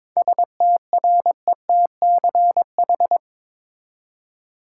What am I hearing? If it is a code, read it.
Morse code, heard as a keyed tone, about 22 wpm, 700 Hz. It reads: STRETCH
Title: Seven Letter Words - 22wpm